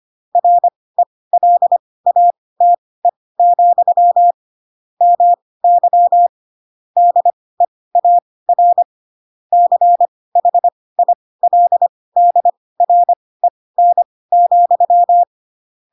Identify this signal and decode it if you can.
Morse code, heard as a keyed tone, about 25 wpm, spaced as for 18 wpm, 700 Hz. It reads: RELATE, MY DEAR CHILDREN,
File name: Pinocchio / Pinocchio ch4 12 25 ik1hge